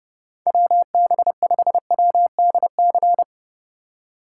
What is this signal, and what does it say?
Morse code, heard as a keyed tone, about 30 wpm, 700 Hz. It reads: WB5WDC
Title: USA Call Signs - 6 Characters - 4x Character Spacing - 30wpm